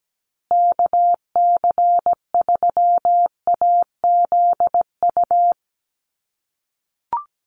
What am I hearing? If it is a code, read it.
Morse code, heard as a keyed tone, about 17 wpm, 700 Hz. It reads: KC3AZU
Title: USA Call Signs Repeated 2x 17wpm